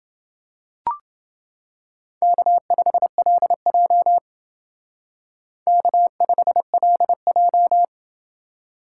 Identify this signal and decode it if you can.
Morse code, heard as a keyed tone, about 28 wpm, 700 Hz. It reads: K5LJ K5LJ